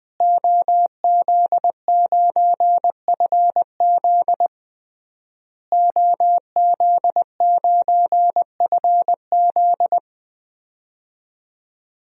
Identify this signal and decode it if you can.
Morse code, heard as a keyed tone, about 20 wpm, 700 Hz. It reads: OZ9FZ OZ9FZ
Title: DXCC-100 Call Signs - 5 Characters - Repeated 3x - 20wpm